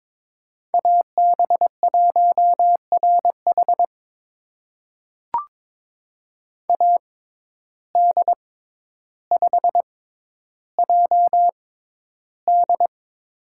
Morse code, heard as a keyed tone, about 22 wpm, spaced as for 11 wpm, 700 Hz. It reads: AB1RH AD5JD
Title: USA Call Signs - 5 Characters - 6x Character Spacing - 22wpm